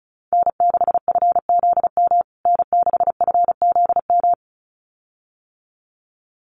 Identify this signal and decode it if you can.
Morse code, heard as a keyed tone, about 34 wpm, 700 Hz. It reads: N6FZM N6FZM
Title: USA Call Signs Repeated 2x 35wpm